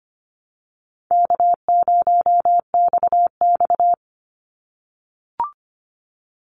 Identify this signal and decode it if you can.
Morse code, heard as a keyed tone, about 25 wpm, 700 Hz. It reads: K0XX
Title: Easy Call Signs - 25wpm